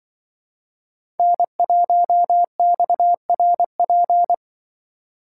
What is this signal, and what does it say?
Morse code, heard as a keyed tone, about 25 wpm, 700 Hz. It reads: N1XRP
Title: Speed Racing - USA 1x3 Call Signs 20wpm